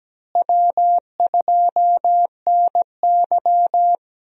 Morse code, heard as a keyed tone, about 17 wpm, 700 Hz. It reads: W2NY
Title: USA Call Signs - 4 Characters - Repeated 3x - 17wpm